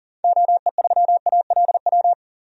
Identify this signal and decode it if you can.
Morse code, heard as a keyed tone, about 40 wpm, 700 Hz. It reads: OE3ALW
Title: DXCC-100 Call Signs - 6 Characters - 40wpm